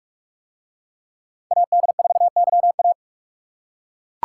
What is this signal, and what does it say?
Morse code, heard as a keyed tone, about 41 wpm, 700 Hz. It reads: AD4YA